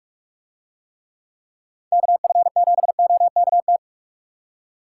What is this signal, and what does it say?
Morse code, heard as a keyed tone, about 40 wpm, 700 Hz. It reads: KF7OKT